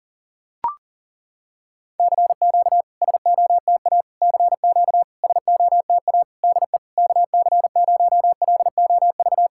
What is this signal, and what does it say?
Morse code, heard as a keyed tone, about 40 wpm, 700 Hz. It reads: CQ SOTA CQ SOTA DE KC0LOV
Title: SOTA QSO Practice 40wpm